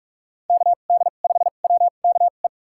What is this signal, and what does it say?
Morse code, heard as a keyed tone, about 46 wpm, spaced as for 32 wpm, 700 Hz. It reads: KD5WKE
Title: USA Call Signs - 6 Characters - 2x Character Spacing - 45wpm